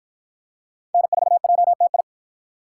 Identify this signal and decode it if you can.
Morse code, heard as a keyed tone, about 50 wpm, 700 Hz. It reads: N4JTI